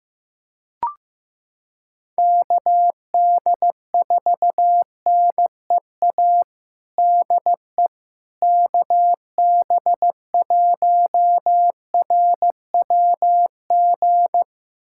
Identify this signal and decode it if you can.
Morse code, heard as a keyed tone, about 15 wpm, 700 Hz. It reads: KD4NEA DE KB1RWG
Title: USA Callsign DE Callsign - 6 Characters - 15wpm